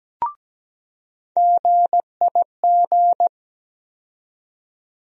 Morse code, heard as a keyed tone, about 17 wpm, 700 Hz. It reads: GIG
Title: USA Call Sign Postfixes - 3 Letters 17wpm